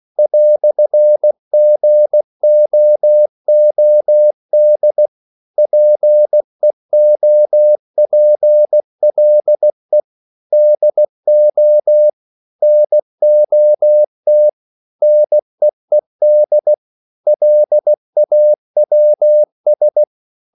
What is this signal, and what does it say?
Morse code, heard as a keyed tone, about 16 wpm, 600 Hz. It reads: "GOOD PEOPLE DO NOT NEED LAWS
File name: CW / Practice / EnglishQuotations0116